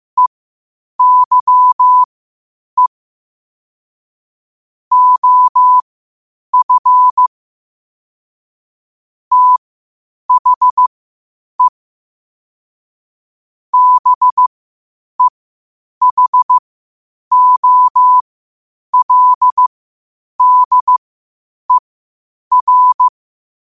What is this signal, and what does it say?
Morse code, heard as a keyed tone, about 15 wpm, spaced as for 8 wpm, 1000 Hz. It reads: EYE OF THE BEHOLDER
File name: qotdfiles / 8wpm